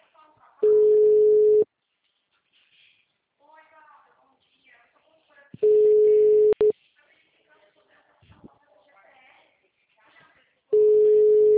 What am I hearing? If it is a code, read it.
Ringback tone (Europe, 425 Hz), with speech.
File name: out- -2055- - - .6496